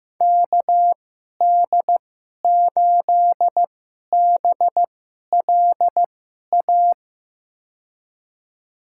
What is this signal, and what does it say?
Morse code, heard as a keyed tone, about 15 wpm, spaced as for 11 wpm, 700 Hz. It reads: KD8BLA
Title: USA Call Signs - 6 Characters - 2x Character Spacing - 15wpm